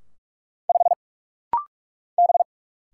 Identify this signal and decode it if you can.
Morse code, heard as a keyed tone, about 45 wpm, 700 Hz. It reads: 5 B
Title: Single Letter-Number Alt - Rapid-Fire 45wpm